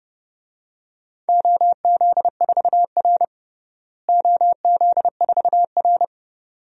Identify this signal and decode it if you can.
Morse code, heard as a keyed tone, about 30 wpm, 700 Hz. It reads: OZ4R OZ4R
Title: DXCC-100 Call Signs - 4 Characters - Repeated 2x - 30wpm